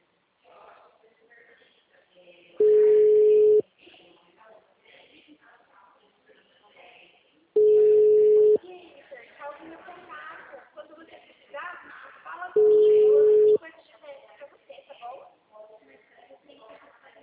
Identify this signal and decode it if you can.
Ringback tone (Europe, 425 Hz), with speech.